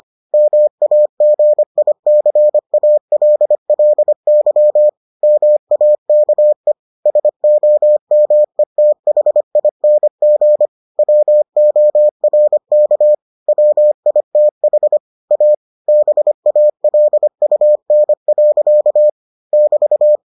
Morse code, heard as a keyed tone, about 25 wpm, 600 Hz. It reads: MAGICALLY MAKE SOMETHING WORK WITH A BALUN. <BT>